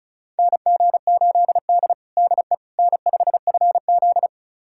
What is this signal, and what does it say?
Morse code, heard as a keyed tone, about 36 wpm, 700 Hz. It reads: NG8D DE N5FZ
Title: USA Callsign DE Callsign - 4 Characters - 35wpm